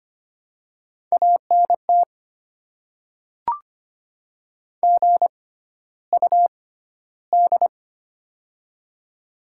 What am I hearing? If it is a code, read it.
Morse code, heard as a keyed tone, about 25 wpm, spaced as for 13 wpm, 700 Hz. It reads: ANT GUD
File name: Top 20 QSO Elements - 6x Character Spacing - 25wpm